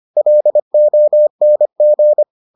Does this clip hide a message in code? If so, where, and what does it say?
Yes: Morse code from 0.2 to 2.2 s, heard as a keyed tone, about 25 wpm, 600 Hz. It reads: LONG